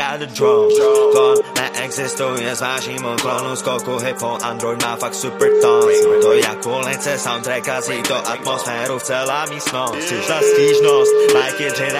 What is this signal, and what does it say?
Ringback tone (Europe, 425 Hz), with speech.